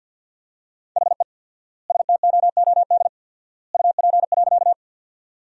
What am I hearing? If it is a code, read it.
Morse code, heard as a keyed tone, about 48 wpm, 700 Hz. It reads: HE STOOD UP.
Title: 4x Word-Spacing - Sentences from Top 500 Words - 50wpm